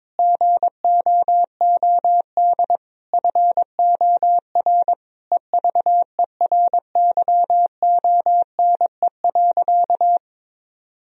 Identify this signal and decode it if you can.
Morse code, heard as a keyed tone, about 22 wpm, 700 Hz. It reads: GOOD FOR EVERYONE.
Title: Sentences from Top 1000 Words - Review Part 4 - 22wpm